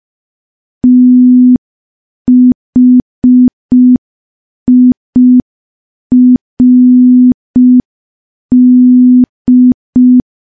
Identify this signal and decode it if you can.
Morse code, heard as a keyed tone, about 5 wpm, 250 Hz. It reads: THIRD
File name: morse / morse-training-05WPM